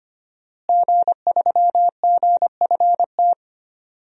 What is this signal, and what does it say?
Morse code, heard as a keyed tone, about 25 wpm, 700 Hz. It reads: G3GFT
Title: DXCC-100 Call Signs - 5 Characters - Repeated 2x - 25wpm